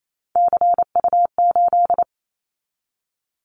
Morse code, heard as a keyed tone, about 28 wpm, 700 Hz. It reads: CU8
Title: Callsign Prefixes for Top 100 DXCC - 28wpm